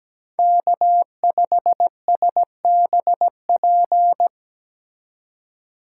Morse code, heard as a keyed tone, about 20 wpm, 700 Hz. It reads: K5SBP
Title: Speed Racing - USA Call Signs 17wpm